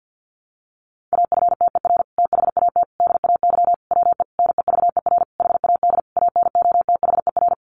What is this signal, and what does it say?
Morse code, heard as a keyed tone, about 48 wpm, 700 Hz. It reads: AFTER THAT DAY WE NEVER HAD ANOTHER
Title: Sentences from Top 1100 Words 50wpm